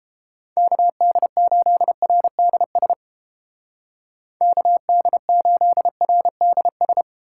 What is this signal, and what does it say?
Morse code, heard as a keyed tone, about 32 wpm, 700 Hz. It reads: KD8RDS KD8RDS